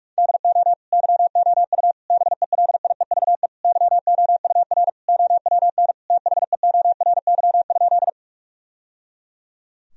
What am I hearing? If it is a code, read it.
Morse code, heard as a keyed tone, about 46 wpm, 700 Hz. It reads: DO YOU BELIEVE YOUR OWN THEORY?
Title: Sentences from Top 1000 Words - Review Part 3 - 45wpm